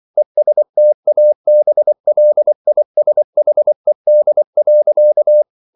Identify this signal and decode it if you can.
Morse code, heard as a keyed tone, about 24 wpm, 600 Hz. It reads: ESTABLISHED.